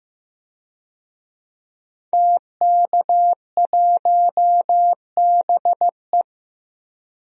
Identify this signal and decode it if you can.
Morse code, heard as a keyed tone, about 15 wpm, 700 Hz. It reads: TK1BE
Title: DXCC-100 Call Signs - 5 Characters - 15wpm